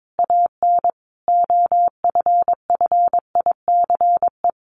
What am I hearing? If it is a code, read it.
Morse code, heard as a keyed tone, about 22 wpm, 700 Hz. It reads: AN OFFICE
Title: Sentences from Top 1000 Words 22wpm